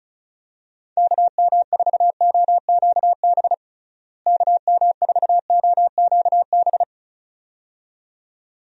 Morse code, heard as a keyed tone, about 36 wpm, 700 Hz. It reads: KM4OQB KM4OQB
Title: USA Call Signs - 6 Characters - Repeated 2x - 35wpm